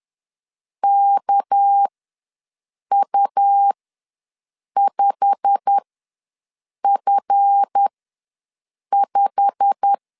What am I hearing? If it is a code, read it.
Morse code, heard as a keyed tone, about 10 wpm, spaced as for 6 wpm, 800 Hz. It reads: KU5F5
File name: LEZIONI / 8 YJ45 / 8 YJ45 ws 10wpm 3843